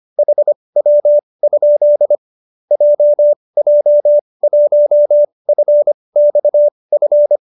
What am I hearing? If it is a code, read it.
Morse code, heard as a keyed tone, about 25 wpm, spaced as for 20 wpm, 590 Hz. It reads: HW? JJ1FXF